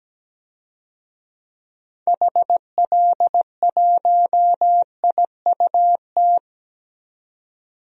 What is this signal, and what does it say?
Morse code, heard as a keyed tone, about 17 wpm, 700 Hz. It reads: HL1IUT
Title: DXCC-100 Call Signs - 6 Characters - 17wpm